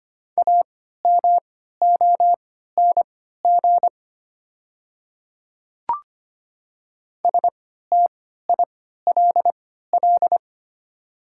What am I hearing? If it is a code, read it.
Morse code, heard as a keyed tone, about 25 wpm, spaced as for 14 wpm, 700 Hz. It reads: AMONG STILL
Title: 3x Character Spacing - Top 500 Words - 25wpm